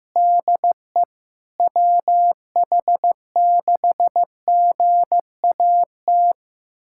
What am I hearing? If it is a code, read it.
Morse code, heard as a keyed tone, about 15 wpm, 700 Hz. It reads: DE WH6GAT
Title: USA Callsign DE Callsign 15wpm